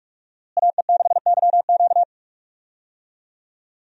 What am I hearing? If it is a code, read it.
Morse code, heard as a keyed tone, about 46 wpm, 700 Hz. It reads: AE6YQ
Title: USA Call Signs - 5 Characters - Repeated 3x - 45wpm